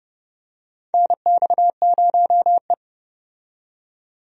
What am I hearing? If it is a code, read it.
Morse code, heard as a keyed tone, about 31 wpm, 700 Hz. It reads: NX0E
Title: USA Call Signs - 4 Characters - 30wpm